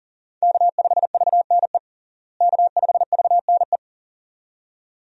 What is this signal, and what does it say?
Morse code, heard as a keyed tone, about 40 wpm, 700 Hz. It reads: K5VNE K5VNE
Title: USA Call Signs - 5 Characters - Repeated 2x - 40wpm